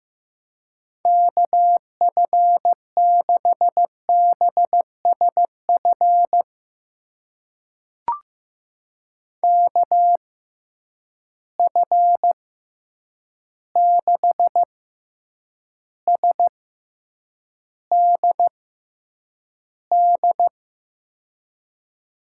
Morse code, heard as a keyed tone, about 15 wpm, spaced as for 8 wpm, 700 Hz. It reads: KF6BSF KF6SDD
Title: USA Call Signs - 6 Characters - 6x Character Spacing - 15wpm